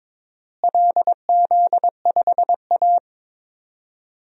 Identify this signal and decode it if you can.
Morse code, heard as a keyed tone, about 21 wpm, 700 Hz. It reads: LZ5A
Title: DXCC-100 Call Signs - 4 Characters - Repeated 3x - 22wpm